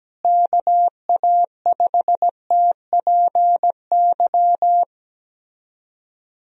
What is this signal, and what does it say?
Morse code, heard as a keyed tone, about 17 wpm, 700 Hz. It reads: KA5TPY
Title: USA Call Signs - 6 Characters - Repeated 3x - 17wpm